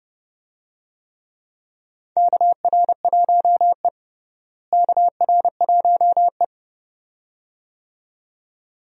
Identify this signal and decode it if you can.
Morse code, heard as a keyed tone, about 30 wpm, 700 Hz. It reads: KR1E KR1E
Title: USA Call Signs - 4 Characters - Repeated 2x - 30wpm